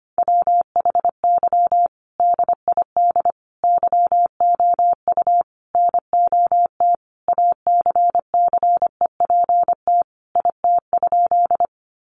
Morse code, heard as a keyed tone, about 25 wpm, 700 Hz. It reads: WHY DID YOU NOT ACCEPT IT?
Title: Sentences from Top 1300 Words 25wpm